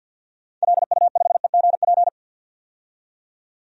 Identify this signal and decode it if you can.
Morse code, heard as a keyed tone, about 52 wpm, 700 Hz. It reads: LU5EGP
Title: DXCC-100 Call Signs - 6 Characters - 50wpm